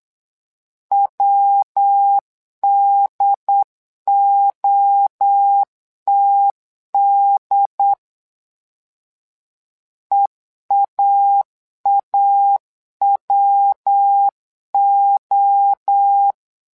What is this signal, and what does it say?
Morse code, heard as a keyed tone, about 8 wpm, 800 Hz. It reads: WDOTD EAAWO
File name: LEZIONI / 3 MWRO / 3 MWRO 8wpm 2606